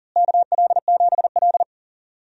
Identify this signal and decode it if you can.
Morse code, heard as a keyed tone, about 39 wpm, 700 Hz. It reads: KL7L
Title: DXCC-100 Call Signs - 4 Characters - Repeated 3x - 40wpm